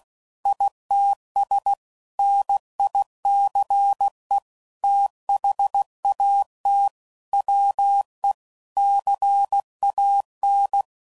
Morse code, heard as a keyed tone, about 16 wpm, 800 Hz. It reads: ITS NICE THAT WE CAN
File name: CW Code Groups @15WPM / QSOs / WD0FBT